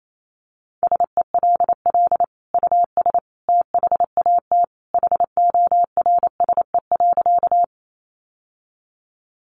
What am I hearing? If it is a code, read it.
Morse code, heard as a keyed tone, about 28 wpm, 700 Hz. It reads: SELL US THAT HORSE.
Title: Sentences from Top 1100 Words 28wpm